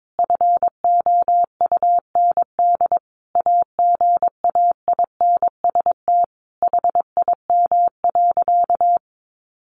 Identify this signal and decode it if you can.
Morse code, heard as a keyed tone, about 22 wpm, 700 Hz. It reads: FOUND AGAINST HIM.
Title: Sentences from Top 1300 Words 22wpm